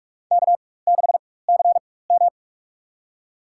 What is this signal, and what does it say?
Morse code, heard as a keyed tone, about 42 wpm, spaced as for 20 wpm, 700 Hz. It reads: K6CM
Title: USA Call Signs - 4 Characters - 4x Character Spacing - 45wpm